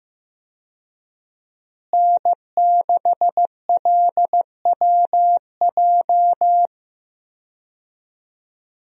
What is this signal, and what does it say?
Morse code, heard as a keyed tone, about 16 wpm, 700 Hz. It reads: N6LWJ
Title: USA Call Signs - 5 Characters - Repeated 3x - 15wpm